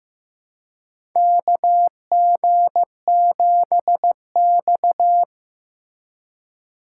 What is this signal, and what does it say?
Morse code, heard as a keyed tone, about 15 wpm, 700 Hz. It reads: KG7X
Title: USA Call Signs - 4 Characters - 3x Character Spacing - 15wpm